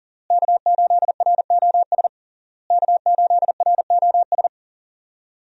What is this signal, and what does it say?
Morse code, heard as a keyed tone, about 40 wpm, 700 Hz. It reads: K8ROS K8ROS
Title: USA Call Signs - 5 Characters - Repeated 3x - 40wpm